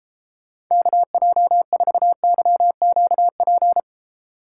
Morse code, heard as a keyed tone, about 36 wpm, 700 Hz. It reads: KJ4YQP